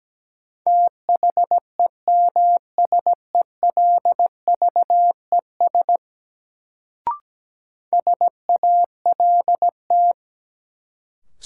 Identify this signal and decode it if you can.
Morse code, heard as a keyed tone, about 17 wpm, 700 Hz. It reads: THEMSELVES SALT
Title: Sentences from Top 800 Words - New Word Drill - 17wpm